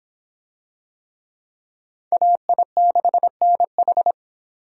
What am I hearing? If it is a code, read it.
Morse code, heard as a keyed tone, about 26 wpm, 700 Hz. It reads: AI6NH